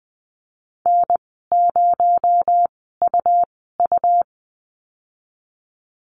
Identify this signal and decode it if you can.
Morse code, heard as a keyed tone, about 20 wpm, spaced as for 15 wpm, 700 Hz. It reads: N0UU
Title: 219 Lesson 35g - Number 0 - Call Signs 20wpm